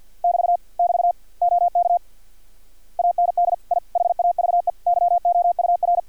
Morse code, heard as a keyed tone, about 50 wpm, 700 Hz. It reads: <BT> <BT> OK AND I HAVE YOUR
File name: archived / CRSnet